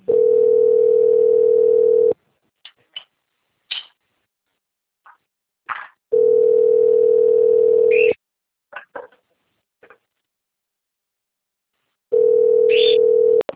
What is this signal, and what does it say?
Ringback tone (North America), 440 + 480 Hz.